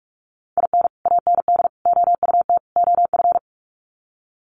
Morse code, heard as a keyed tone, about 44 wpm, 700 Hz. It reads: IN AND OUT OF
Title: Sets of 4 Words 45wpm